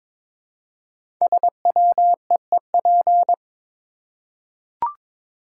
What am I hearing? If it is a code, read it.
Morse code, heard as a keyed tone, about 22 wpm, 700 Hz. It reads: SWEEP
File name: 5 Letter Words - First 3 Letters Given - 22wpm